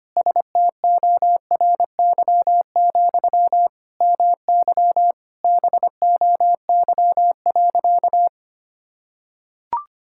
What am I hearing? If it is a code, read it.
Morse code, heard as a keyed tone, about 25 wpm, 700 Hz. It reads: STORY, MY BOY.
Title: Sentences from Top 1000 Words - Review Part 6 - 25wpm